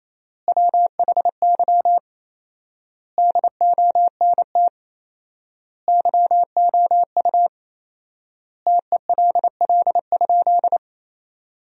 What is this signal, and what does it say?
Morse code, heard as a keyed tone, about 28 wpm, 700 Hz. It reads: WHY DONT YOU TELL?
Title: 4x Word-Spacing - Sentences from Top 300 Words - 28wpm